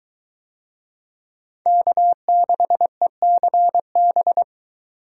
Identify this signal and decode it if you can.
Morse code, heard as a keyed tone, about 23 wpm, 700 Hz. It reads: K6ECB